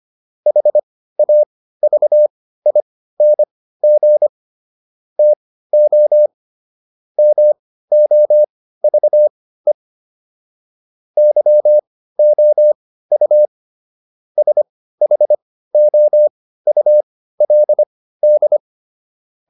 Morse code, heard as a keyed tone, about 25 wpm, spaced as for 15 wpm, 600 Hz. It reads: HAVING TO MOVE YOU SHOULD